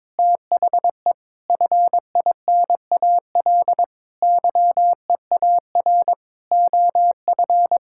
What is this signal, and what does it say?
Morse code, heard as a keyed tone, about 22 wpm, 700 Hz. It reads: THE FINAL YEAR OF
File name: practice / 22wpm / Sets of 8 Words - 22wpm